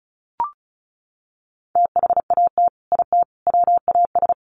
Morse code, heard as a keyed tone, about 34 wpm, 700 Hz. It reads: THAT IT WAS
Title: Sets of 3 Words 35wpm